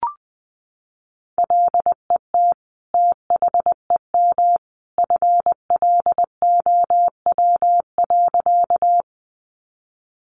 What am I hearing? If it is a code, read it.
Morse code, heard as a keyed tone, about 20 wpm, 700 Hz. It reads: LET THEM FLOW.
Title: Sentences from Top 1200 Words 20wpm